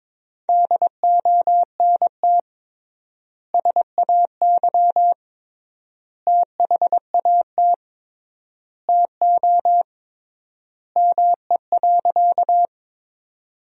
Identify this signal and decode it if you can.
Morse code, heard as a keyed tone, about 22 wpm, 700 Hz. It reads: DONT SAY THAT TO ME.
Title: 3x Word-Spacing - Sentences from Top 300 Words - 22wpm